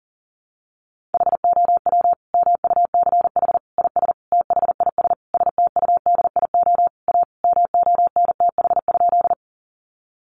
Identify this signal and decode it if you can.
Morse code, heard as a keyed tone, about 37 wpm, 700 Hz. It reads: HOW MUCH IS THIS STUDIO A MONTH?
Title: Sentences from Top 1200 Words 40wpm